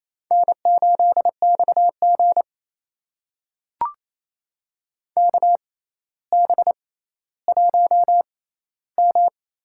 Morse code, heard as a keyed tone, about 28 wpm, spaced as for 14 wpm, 700 Hz. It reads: N8XG KB1M
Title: USA Call Signs - 6x Character Spacing - 28wpm